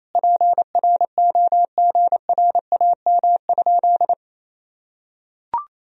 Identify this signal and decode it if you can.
Morse code, heard as a keyed tone, about 28 wpm, 700 Hz. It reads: PROGRAM?
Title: Sentences from Top 1000 Words - Review Part 4 - 28wpm